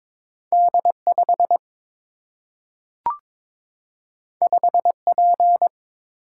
Morse code, heard as a keyed tone, about 22 wpm, 700 Hz. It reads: D 5 5 P
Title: Single-Letter Single-Number 22wpm